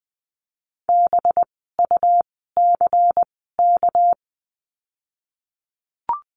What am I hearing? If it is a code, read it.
Morse code, heard as a keyed tone, about 20 wpm, spaced as for 15 wpm, 700 Hz. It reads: BUCK
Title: 199 Lesson 32f - Letter k - Words 20wpm